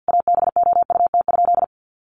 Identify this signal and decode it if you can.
Morse code, heard as a keyed tone, about 50 wpm, 700 Hz. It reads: ABOUT?
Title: Sentences from Top 1100 Words 50wpm